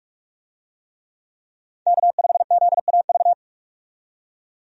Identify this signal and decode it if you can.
Morse code, heard as a keyed tone, about 46 wpm, 700 Hz. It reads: K5ZAV
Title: USA Call Signs - 5 Characters - 45wpm